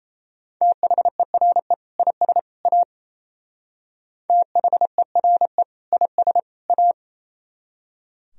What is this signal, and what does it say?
Morse code, heard as a keyed tone, about 34 wpm, 700 Hz. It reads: THERE IS A THERE IS A
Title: Speed-Racing - Sets of 3 Words 28wpm